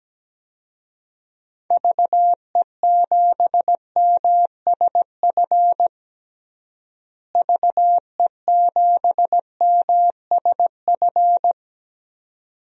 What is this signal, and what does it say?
Morse code, heard as a keyed tone, about 17 wpm, 700 Hz. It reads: VE7MSF VE7MSF
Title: DXCC-100 Call Signs - 6 Characters - Repeated 2x - 17wpm